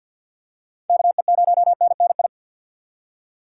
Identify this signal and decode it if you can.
Morse code, heard as a keyed tone, about 50 wpm, 700 Hz. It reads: KE0NNI
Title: USA Call Signs - 6 Characters - 6x Character Spacing - 50wpm